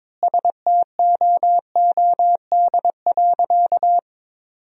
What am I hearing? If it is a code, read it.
Morse code, heard as a keyed tone, about 22 wpm, 700 Hz. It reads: STOOD.
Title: 2x Word-Spacing - Sentences from Top 500 Words - 22wpm